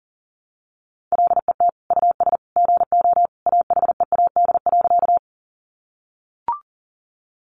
Morse code, heard as a keyed tone, about 40 wpm, 700 Hz. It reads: LET US GO AHEAD.
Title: Sentences from Top 1100 Words 40wpm